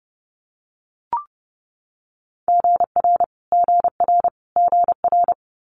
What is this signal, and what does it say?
Morse code, heard as a keyed tone, about 30 wpm, 700 Hz. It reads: GR GR GR
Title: Multi-letter Phonemes Repeated 3x 30wpm